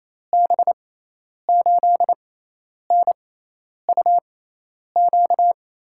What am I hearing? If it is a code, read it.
Morse code, heard as a keyed tone, about 28 wpm, spaced as for 10 wpm, 700 Hz. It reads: B8NUQ
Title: USA Call Signs - 6 Characters - 6x Character Spacing - 28wpm